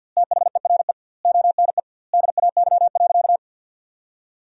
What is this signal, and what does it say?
Morse code, heard as a keyed tone, about 50 wpm, 700 Hz. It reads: THERE ONE DAY.
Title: 2x Word-Spacing - Sentences from Top 300 Words - 50wpm